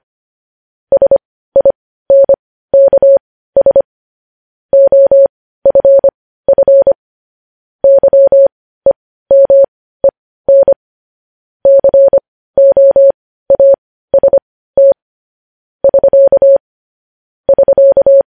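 Morse code, heard as a keyed tone, about 25 wpm, spaced as for 15 wpm, 550 Hz. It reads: SINKS OFF YEMEN COAST <SK> <SK>
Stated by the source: Morse news headlines practice sent at 15 WPM.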